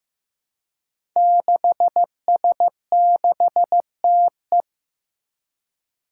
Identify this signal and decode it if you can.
Morse code, heard as a keyed tone, about 15 wpm, 700 Hz. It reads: 6S6TE
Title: Five Challenging Letter-Number Combinations 15wpm